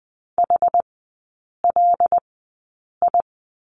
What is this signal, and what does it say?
Morse code, heard as a keyed tone, about 20 wpm, spaced as for 8 wpm, 690 Hz. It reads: HLI
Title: 081 Lesson 14e - Number 2 - Three Characters 20wpm